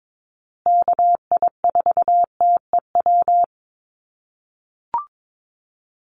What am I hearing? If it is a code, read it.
Morse code, heard as a keyed tone, about 22 wpm, 700 Hz. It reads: KI4TEW
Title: USA Call Signs Repeated 3x 22wpm